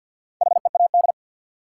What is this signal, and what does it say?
Morse code, heard as a keyed tone, about 48 wpm, 700 Hz. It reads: HEAD
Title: Sentences from Top 300 Words - New Word Drill - 50wpm